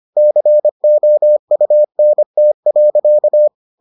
Morse code, heard as a keyed tone, about 25 wpm, 600 Hz. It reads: COUNT.